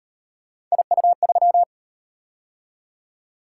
Morse code, heard as a keyed tone, about 36 wpm, 700 Hz. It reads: IU3